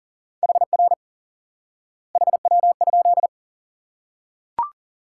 Morse code, heard as a keyed tone, about 40 wpm, 700 Hz. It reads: HR HW?
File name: IQE Course 3 - Lesson 89 - YAGI - Two Elements - 40wpm